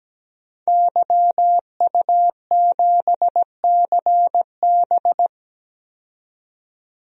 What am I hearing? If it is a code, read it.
Morse code, heard as a keyed tone, about 17 wpm, 700 Hz. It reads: YU7CB
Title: DXCC-100 Call Signs - 5 Characters - 17wpm